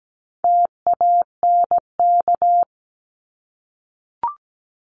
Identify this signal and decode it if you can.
Morse code, heard as a keyed tone, about 17 wpm, 700 Hz. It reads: TANK
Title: Top 1751-2000 Words 17wpm